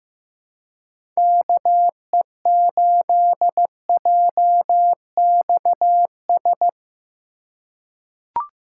Morse code, heard as a keyed tone, about 15 wpm, 700 Hz. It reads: KE8JXS
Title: USA Call Signs - 6 Characters - 4x Character Spacing - 15wpm